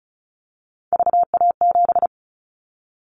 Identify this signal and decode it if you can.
Morse code, heard as a keyed tone, about 34 wpm, 700 Hz. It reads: VA7
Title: Callsign Prefixes for Top 100 DXCC - 35wpm